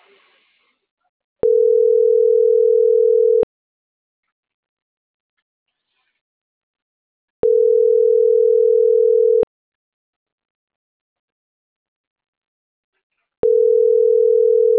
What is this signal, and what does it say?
Ringback tone (North America), 440 + 480 Hz.